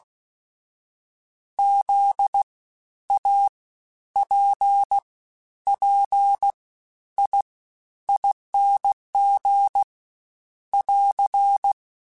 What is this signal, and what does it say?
Morse code, heard as a keyed tone, about 16 wpm, spaced as for 10 wpm, 800 Hz. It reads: ZAPPIING <AR>
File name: Special INGs-mp3s / Special INGs-mp3s / Fast ing @15